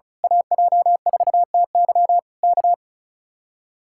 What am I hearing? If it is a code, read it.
Morse code, heard as a keyed tone, about 34 wpm, 700 Hz. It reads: AJ4TY K
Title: Speed-Racing Common QSO Phrases - 35wpm